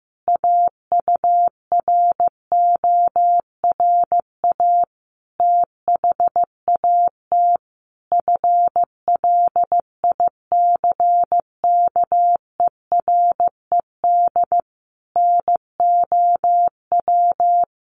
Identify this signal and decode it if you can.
Morse code, heard as a keyed tone, about 15 wpm, 700 Hz. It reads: AURORA THAT FLICKERED NOW
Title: A Sound Of Thunder 15wpm